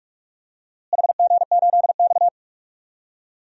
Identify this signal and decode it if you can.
Morse code, heard as a keyed tone, about 46 wpm, 700 Hz. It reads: HG8X
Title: DXCC-100 Call Signs - 4 Characters - Repeated 3x - 45wpm